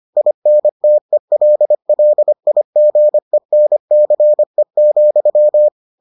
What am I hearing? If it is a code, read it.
Morse code, heard as a keyed tone, about 25 wpm, 600 Hz. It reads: INTELLIGENCE,